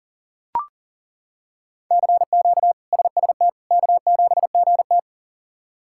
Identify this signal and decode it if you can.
Morse code, heard as a keyed tone, about 40 wpm, 700 Hz. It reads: CQ SST K7GT
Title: K1USN SST QSO Practice - 40wpm